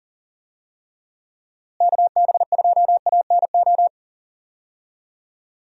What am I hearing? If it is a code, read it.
Morse code, heard as a keyed tone, about 40 wpm, 700 Hz. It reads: KB2ANO
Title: USA Call Signs - 6 Characters - Repeated 3x - 40wpm